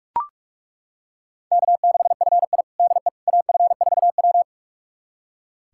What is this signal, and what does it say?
Morse code, heard as a keyed tone, about 41 wpm, 700 Hz. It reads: K6FI DE AF4W
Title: USA Callsign DE Callsign - 4 Characters - 45wpm